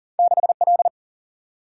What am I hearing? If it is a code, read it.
Morse code, heard as a keyed tone, about 40 wpm, 700 Hz. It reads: 6 L
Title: Single-Letter Single-Number 40wpm